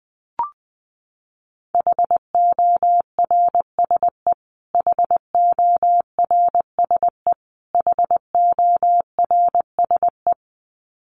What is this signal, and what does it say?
Morse code, heard as a keyed tone, about 20 wpm, 700 Hz. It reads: HORSE HORSE HORSE
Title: Top 500 Words Repeated 20wpm